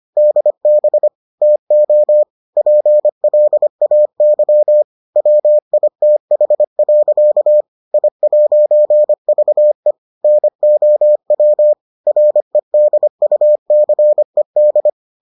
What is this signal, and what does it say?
Morse code, heard as a keyed tone, about 25 wpm, 600 Hz. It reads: DB TO PLAY WITH. I'VE NOW REDUCED